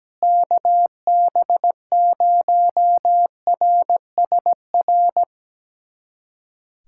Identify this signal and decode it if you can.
Morse code, heard as a keyed tone, about 17 wpm, 700 Hz. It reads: KB0RSR
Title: USA Callsign DE Callsign - 6 Characters - 17wpm